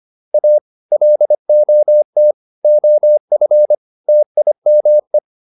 Morse code, heard as a keyed tone, about 25 wpm, 600 Hz. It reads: A LOT OF TIME